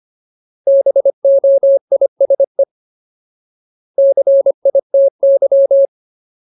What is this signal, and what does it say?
Morse code, heard as a keyed tone, about 25 wpm, 550 Hz. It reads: BOISE CITY
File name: Ultimate Word List / 25WPM Ultimate